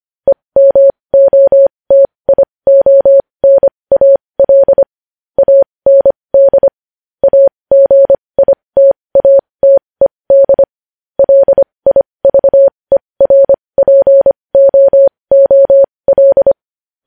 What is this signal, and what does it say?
Morse code, heard as a keyed tone, about 25 wpm, spaced as for 20 wpm, 550 Hz. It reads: EMOTIONAL AND AGITATED LIVERPOOL
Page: News Headlines in Morse Code at 20 WPM